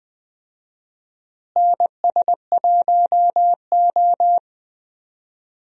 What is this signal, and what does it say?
Morse code, heard as a keyed tone, about 21 wpm, 700 Hz. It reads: NS1O
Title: USA Call Signs - 4 Characters - Repeated 3x - 20wpm